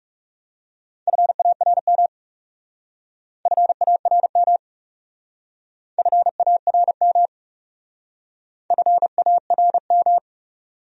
Morse code, heard as a keyed tone, about 37 wpm, 700 Hz. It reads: FARM FARM FARM FARM
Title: Speed-Racing Top 500 Words 30wpm